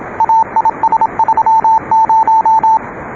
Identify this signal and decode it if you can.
Morse code, heard as a keyed tone, about 26 wpm, 880 Hz. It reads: AIS30